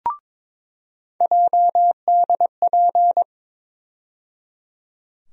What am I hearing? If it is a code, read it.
Morse code, heard as a keyed tone, about 22 wpm, 700 Hz. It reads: JDP
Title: USA Call Sign Postfixes - 3 Letters 22wpm